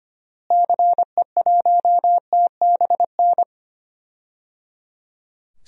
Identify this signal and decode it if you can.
Morse code, heard as a keyed tone, about 25 wpm, 700 Hz. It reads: CE1TBN
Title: DXCC-100 Call Signs - 6 Characters - Repeated 3x - 25wpm